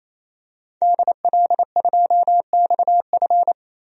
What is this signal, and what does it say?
Morse code, heard as a keyed tone, about 28 wpm, 700 Hz. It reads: DL2XF
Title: DXCC-100 Call Signs - 5 Characters - 28wpm